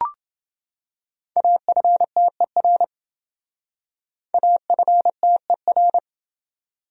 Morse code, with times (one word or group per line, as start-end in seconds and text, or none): 1.4-2.8 AFTER
4.3-6.0 AFTER